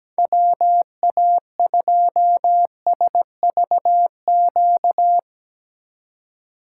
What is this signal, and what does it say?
Morse code, heard as a keyed tone, about 18 wpm, 700 Hz. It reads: WA2SVQ
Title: Speed Racing - USA 2x3 Call Signs 17wpm